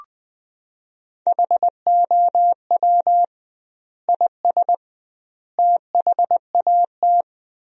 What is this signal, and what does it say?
Morse code, heard as a keyed tone, about 21 wpm, 700 Hz. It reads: HOW IS THAT
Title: 2x Word-Spacing - Sentences from Top 500 Words - 20wpm